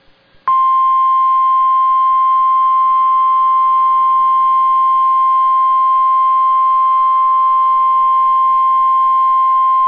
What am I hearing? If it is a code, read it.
Emergency Alert System (SAME) tones (the 1050 Hz attention tone). It is heard from 0.8 to 9.6 s.